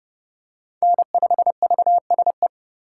Morse code, heard as a keyed tone, about 30 wpm, 700 Hz. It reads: N5VSE
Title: USA Call Signs - 5 Characters - 4x Character Spacing - 30wpm